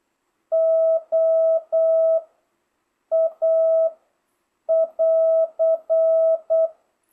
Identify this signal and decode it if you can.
Morse code, heard as a keyed tone, about 8 wpm, 650 Hz. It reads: O A <AR>